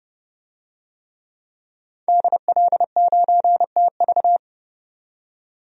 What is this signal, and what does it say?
Morse code, heard as a keyed tone, about 30 wpm, 700 Hz. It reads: DL9TV
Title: DXCC-100 Call Signs - 5 Characters - 30wpm